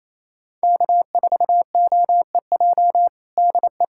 Morse code, heard as a keyed tone, about 28 wpm, 700 Hz. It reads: K4OEJ DE
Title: USA Callsign DE Callsign - 5 Characters - 28wpm